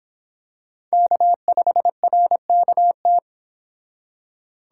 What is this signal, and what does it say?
Morse code, heard as a keyed tone, about 26 wpm, 700 Hz. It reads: K5RKT